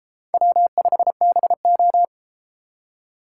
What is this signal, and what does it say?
Morse code, heard as a keyed tone, about 32 wpm, 700 Hz. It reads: W5BO